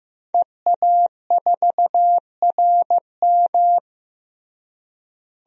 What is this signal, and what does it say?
Morse code, heard as a keyed tone, about 15 wpm, 700 Hz. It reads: EA4RM
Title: DXCC-100 Call Signs - 5 Characters - Repeated 3x - 15wpm